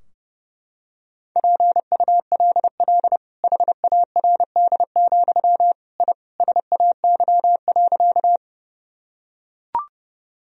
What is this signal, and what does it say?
Morse code, heard as a keyed tone, about 30 wpm, 700 Hz. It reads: PULL HARD, I SAY.
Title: Sentences from Top 1000 Words - Review Part 2 - 30wpm